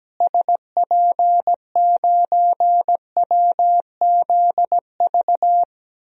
Morse code, heard as a keyed tone, about 17 wpm, 700 Hz. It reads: SP9WZV
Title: DXCC-100 Call Signs - 6 Characters - Repeated 2x - 17wpm